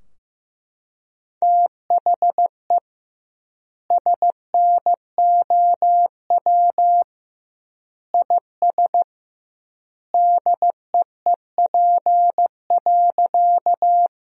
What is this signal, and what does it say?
Morse code, heard as a keyed tone, about 15 wpm, 700 Hz. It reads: THE SNOW IS DEEP.
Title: 2x Word-Spacing - Sentences from Top 500 Words - 15wpm